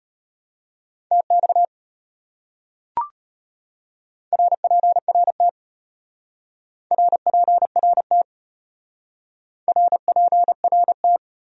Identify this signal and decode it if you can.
Morse code, heard as a keyed tone, about 33 wpm, 700 Hz. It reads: TX RPRT RPRT RPRT